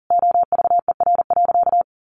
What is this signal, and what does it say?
Morse code, heard as a keyed tone, about 40 wpm, 700 Hz. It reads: OVER.
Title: Sentences from Top 1200 Words 40wpm